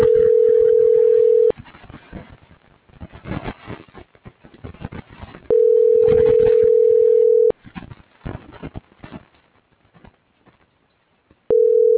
Ringback tone (North America), 440 + 480 Hz.